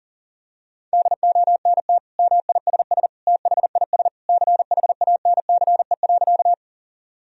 Morse code, heard as a keyed tone, about 40 wpm, 700 Hz. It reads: DONT MISS THIS CHANCE.
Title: Sentences from Top 1000 Words - Review Part 3 - 40wpm